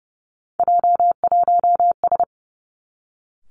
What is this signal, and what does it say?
Morse code, heard as a keyed tone, about 30 wpm, 700 Hz. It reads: J1S
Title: Three Challenging Letter-Number Combinations 30wpm